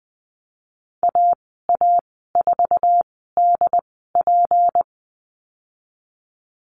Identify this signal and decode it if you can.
Morse code, heard as a keyed tone, about 20 wpm, spaced as for 14 wpm, 700 Hz. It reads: AA4DP
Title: 193 Lesson 31g - Letter v - Call Signs 20wpm